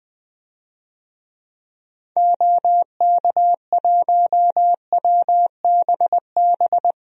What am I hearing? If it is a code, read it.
Morse code, heard as a keyed tone, about 20 wpm, 700 Hz. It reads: OK1WBB
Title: DXCC-100 Call Signs - 6 Characters - Repeated 2x - 20wpm